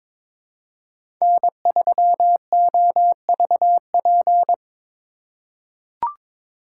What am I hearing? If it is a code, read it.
Morse code, heard as a keyed tone, about 22 wpm, 700 Hz. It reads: N3OVP
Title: USA Call Signs - 5 Characters - 2x Character Spacing - 22wpm